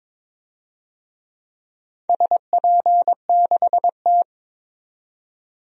Morse code, heard as a keyed tone, about 22 wpm, 700 Hz. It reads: SP6T
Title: DXCC-100 Call Signs - 4 Characters - 22wpm